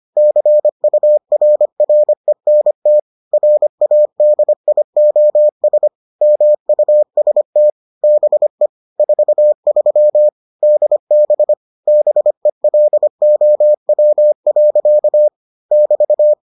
Morse code, heard as a keyed tone, about 25 wpm, 600 Hz. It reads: CURRENT RADIOS MUST BE 43 DB BELOW. <BT>